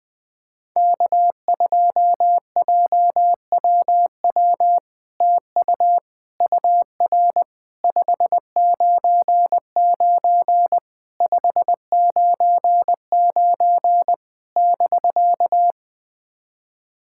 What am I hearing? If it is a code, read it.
Morse code, heard as a keyed tone, about 20 wpm, 700 Hz. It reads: K2JWW TU UR 599 599 <BK>
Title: SOTA QSO Practice 20wpm